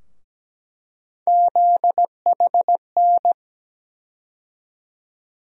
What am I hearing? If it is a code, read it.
Morse code, heard as a keyed tone, about 17 wpm, 700 Hz. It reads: ZHN
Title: USA Call Sign Postfixes - 3 Letters 17wpm